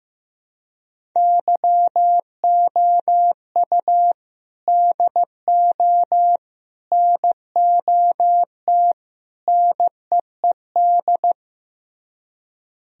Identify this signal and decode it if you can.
Morse code, heard as a keyed tone, about 15 wpm, 700 Hz. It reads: YOU DO NOT NEED
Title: Speed-Racing - Sets of 4 Words 15wpm